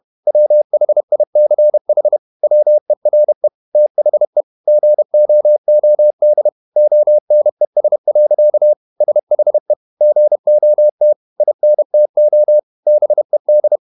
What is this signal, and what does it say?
Morse code, heard as a keyed tone, about 31 wpm, 600 Hz. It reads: WHICH WERE THE GOOD ONES. SHE GOT INTO BED
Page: Never 31wpm | CW med Gnister